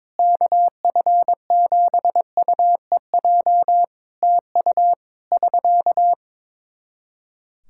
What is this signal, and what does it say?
Morse code, heard as a keyed tone, about 22 wpm, 700 Hz. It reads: KF7UEJ TU <SK>
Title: K1USN SST QSO Practice 22wpm